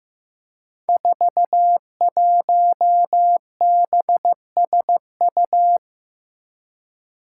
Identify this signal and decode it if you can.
Morse code, heard as a keyed tone, about 15 wpm, 700 Hz. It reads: 41BSU
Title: Five Challenging Letter-Number Combinations 15wpm